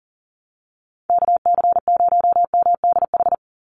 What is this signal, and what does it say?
Morse code, heard as a keyed tone, about 40 wpm, 700 Hz. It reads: KC0MDH
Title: USA Call Signs 40wpm